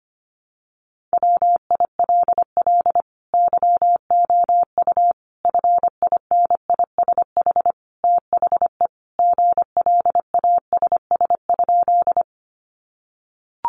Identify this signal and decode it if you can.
Morse code, heard as a keyed tone, about 25 wpm, 700 Hz. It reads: WILL YOU FINISH THE GLASS?
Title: Sentences from Top 1300 Words 25wpm